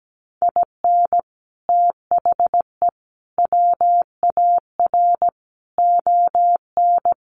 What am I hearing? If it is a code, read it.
Morse code, heard as a keyed tone, about 17 wpm, 700 Hz. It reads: IN THE WAR ON
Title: Sets of 5 Words Encore2 17wpm